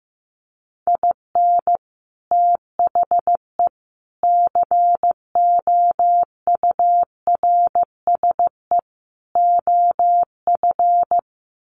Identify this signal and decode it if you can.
Morse code, heard as a keyed tone, about 14 wpm, 700 Hz. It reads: IN THE COURSE OF
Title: Sets of 4 Words 15wpm